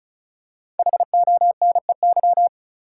Morse code, heard as a keyed tone, about 36 wpm, 700 Hz. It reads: HONEY
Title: Five Letter Words - 35wpm